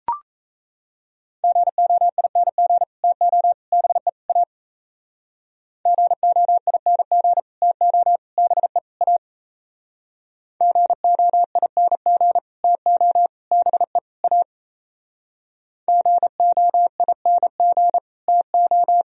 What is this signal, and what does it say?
Morse code, heard as a keyed tone, about 37 wpm, 700 Hz. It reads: GOING TO BE A GOING TO BE A GOING TO BE A GOING TO